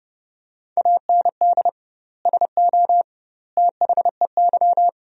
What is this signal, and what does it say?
Morse code, heard as a keyed tone, about 30 wpm, 700 Hz. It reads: AND SO THEY
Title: 2x Word-Spacing - Sentences from Top 400 Words - 30wpm